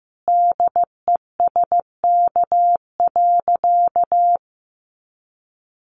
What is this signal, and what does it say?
Morse code, heard as a keyed tone, about 15 wpm, 700 Hz. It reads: DESK.
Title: A Sound Of Thunder 15wpm